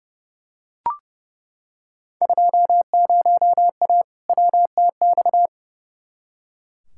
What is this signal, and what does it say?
Morse code, heard as a keyed tone, about 30 wpm, 700 Hz. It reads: 20A WTX
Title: Field Day Exchanges Encore - 30wpm